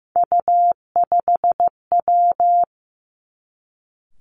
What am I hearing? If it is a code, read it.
Morse code, heard as a keyed tone, about 15 wpm, 700 Hz. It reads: U5W
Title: Three Challenging Letter-Number Combinations 15wpm